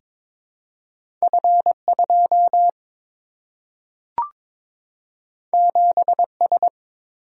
Morse code, heard as a keyed tone, about 23 wpm, 700 Hz. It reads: F2 7S
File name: practice / 22wpm / Two Letter-Number Combinations - 22wpm